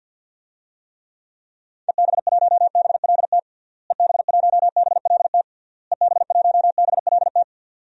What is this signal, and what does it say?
Morse code, heard as a keyed tone, about 52 wpm, 700 Hz. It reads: EB1BLT EB1BLT EB1BLT
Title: DXCC-100 Call Signs - 6 Characters - Repeated 3x - 50wpm